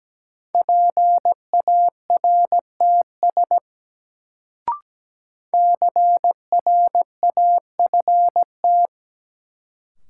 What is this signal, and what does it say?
Morse code, heard as a keyed tone, about 17 wpm, 700 Hz. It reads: PARTS CRAFT
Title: Sentences from Top 1000 Words - Review Part 6 - New Word Drill - 17wpm